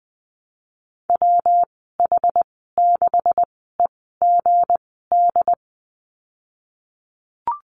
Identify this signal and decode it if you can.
Morse code, heard as a keyed tone, about 20 wpm, spaced as for 15 wpm, 700 Hz. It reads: WH6EGD
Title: 161 Lesson 26g - Letter g - Call Signs 20wpm